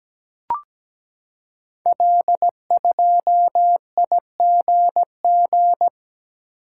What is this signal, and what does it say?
Morse code, heard as a keyed tone, about 17 wpm, 700 Hz. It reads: L2IGG
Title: Code Groups - 5 Letter:Numbers - 17wpm